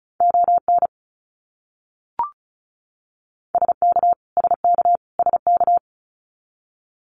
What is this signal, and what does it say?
Morse code, heard as a keyed tone, about 36 wpm, 700 Hz. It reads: ON SK SK SK
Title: Canadian Provinces and Territories - Repeated 3x - 35wpm